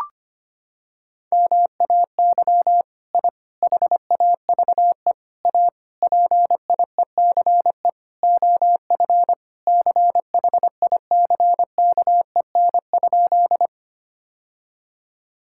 Morse code, heard as a keyed tone, about 25 wpm, 700 Hz. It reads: MAY I HAVE A PIECE OF CHICKEN?
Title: Sentences from Top 1000 Words - Review Part 5 - 25wpm